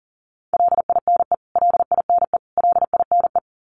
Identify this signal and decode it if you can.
Morse code, heard as a keyed tone, about 40 wpm, 700 Hz. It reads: LINE LINE LINE
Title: Top 200 Words Repeated 40wpm